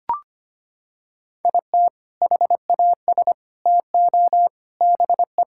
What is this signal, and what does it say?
Morse code, heard as a keyed tone, about 25 wpm, 700 Hz. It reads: IT HAS TO BE
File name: practice / 25wpm / Sentences with 3-Word Phrases - 25wpm